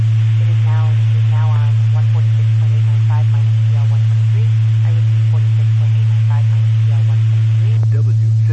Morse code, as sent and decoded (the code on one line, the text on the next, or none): none